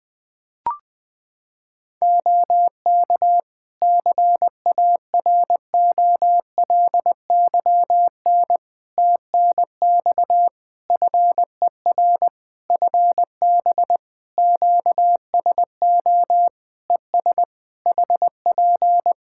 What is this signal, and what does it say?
Morse code, heard as a keyed tone, about 20 wpm, 700 Hz. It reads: OK CAROLYN TNX FER FB QSO ES HP
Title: Practice QSO - Exchange 4 - Ending - 20wpm